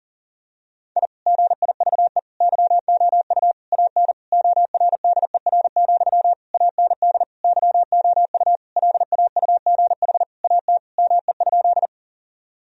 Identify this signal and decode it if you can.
Morse code, heard as a keyed tone, about 37 wpm, 700 Hz. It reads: I GIVE YOU AN ORDER, AND YOU LAUGH AT ME?
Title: Sentences from Top 1000 Words - Review Part 2 - 40wpm